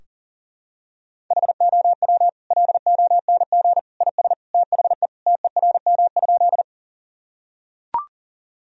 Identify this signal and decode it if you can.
Morse code, heard as a keyed tone, about 40 wpm, 700 Hz. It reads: HOW LONG IS THE TERM?
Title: Sentences from Top 1000 Words - Review Part 4 - 40wpm